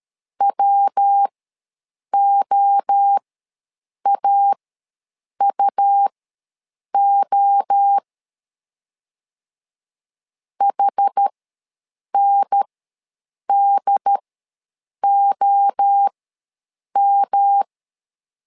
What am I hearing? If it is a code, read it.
Morse code, heard as a keyed tone, about 13 wpm, spaced as for 7 wpm, 800 Hz. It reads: WOAUO HNDOM
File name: LEZIONI / 3 MWRO / 3 MWRO ws 12wpm 1914